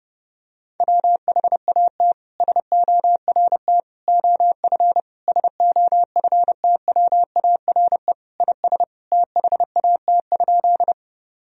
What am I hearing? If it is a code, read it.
Morse code, heard as a keyed tone, about 30 wpm, 700 Hz. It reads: WHAT SORT OF SOFTWARE IS THAT?
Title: Sentences from Top 1000 Words - Review Part 3 - 30wpm